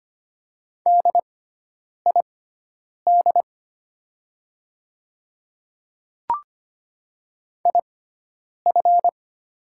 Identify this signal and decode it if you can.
Morse code, heard as a keyed tone, about 25 wpm, spaced as for 9 wpm, 700 Hz. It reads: DID IF
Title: 6x Character Spacing - Top 200 Words - 25wpm